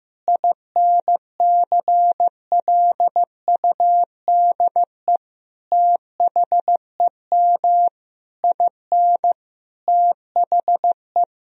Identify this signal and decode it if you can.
Morse code, heard as a keyed tone, about 15 wpm, 700 Hz. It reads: INCLUDE THEM IN THE
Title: Sentences from Top 1000 Words - Review Part 3 - 15wpm